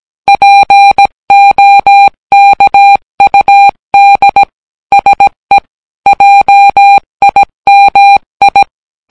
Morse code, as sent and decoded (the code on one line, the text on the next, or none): .--. --- -.- ..- -.. / ... / .--- .. -- ..
POKUD SE JIMI